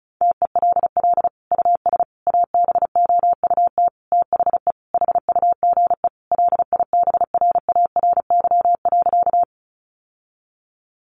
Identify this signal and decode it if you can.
Morse code, heard as a keyed tone, about 34 wpm, 700 Hz. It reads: TELL US ABOUT THE HUGE LIBRARY.
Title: Sentences from Top 1100 Words 35wpm